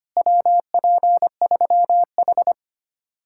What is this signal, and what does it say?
Morse code, heard as a keyed tone, about 31 wpm, 700 Hz. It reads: WP3H
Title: Speed-Racing Common QSO Phrases - 25wpm